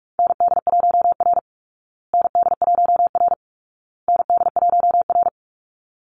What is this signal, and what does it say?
Morse code, heard as a keyed tone, about 44 wpm, 700 Hz. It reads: ND1R ND1R ND1R
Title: Call Signs Repeated 2x - 45wpm